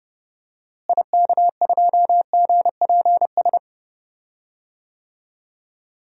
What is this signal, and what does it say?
Morse code, heard as a keyed tone, about 30 wpm, 700 Hz. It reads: IK2GPS
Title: DXCC-100 Call Signs - 6 Characters - Repeated 2x - 30wpm